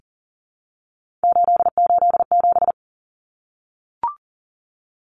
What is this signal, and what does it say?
Morse code, heard as a keyed tone, about 40 wpm, 700 Hz. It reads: 887
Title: Common Cut Numbers - 3-Digit 40wpm